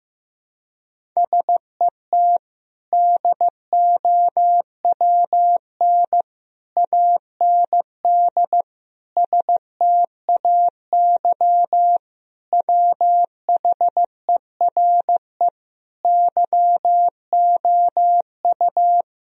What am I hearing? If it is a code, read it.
Morse code, heard as a keyed tone, about 15 wpm, 700 Hz. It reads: SET DOWN AND STAY WHERE YOU
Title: Sentences from Top 1000 Words - Review Part 1 - 15wpm